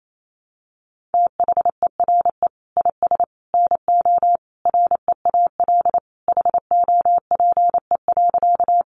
Morse code, heard as a keyed tone, about 28 wpm, 700 Hz. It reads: THERE IS NO REAL HOPE.
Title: Sentences from Top 1000 Words 28wpm